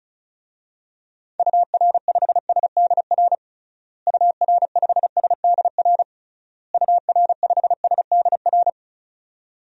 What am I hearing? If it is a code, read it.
Morse code, heard as a keyed tone, about 36 wpm, 700 Hz. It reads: UR5SDR UR5SDR UR5SDR
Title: DXCC-100 Call Signs - 6 Characters - Repeated 3x - 35wpm